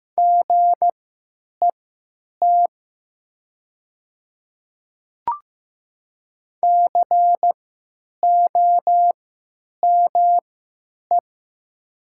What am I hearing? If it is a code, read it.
Morse code, heard as a keyed tone, about 15 wpm, spaced as for 9 wpm, 700 Hz. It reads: GET COME
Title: 3x Character Spacing - Top 200 Words - 15wpm